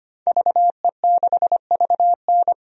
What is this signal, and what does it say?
Morse code, heard as a keyed tone, about 25 wpm, 700 Hz. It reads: VE6VN
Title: DXCC-100 Call Signs - 5 Characters - 25wpm